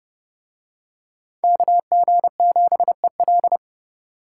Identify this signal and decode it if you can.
Morse code, heard as a keyed tone, about 29 wpm, 700 Hz. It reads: KG7EL